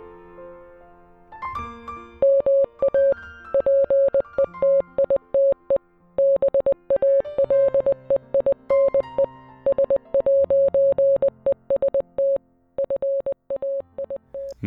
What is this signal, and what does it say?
Morse code, heard as a keyed tone, about 20 wpm, 550 Hz. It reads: MA PETITE BALEINE S'EST FAIT